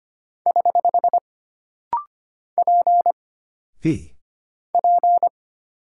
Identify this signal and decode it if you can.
Morse code, heard as a keyed tone, about 25 wpm, 700 Hz. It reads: <HH> P P
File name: Single Character (Letter:Number:Punctuation:Prosign) - Rapid-Fire - 25wpm